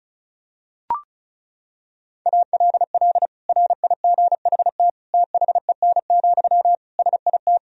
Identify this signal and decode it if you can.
Morse code, heard as a keyed tone, about 36 wpm, 700 Hz. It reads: ALL RIGHT THEN, SIT
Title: Sentences from Top 500 Words and Abbreviations 35wpm